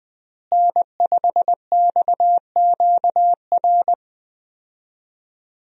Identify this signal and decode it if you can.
Morse code, heard as a keyed tone, about 20 wpm, 700 Hz. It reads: N5XQR
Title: USA Call Signs - 5 Characters - Repeated 2x - 20wpm